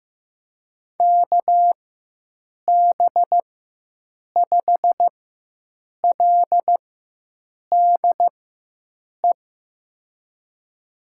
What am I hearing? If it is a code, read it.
Morse code, heard as a keyed tone, about 15 wpm, spaced as for 7 wpm, 700 Hz. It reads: KB5LDE
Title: USA Call Signs - 4x Character Spacing - 15wpm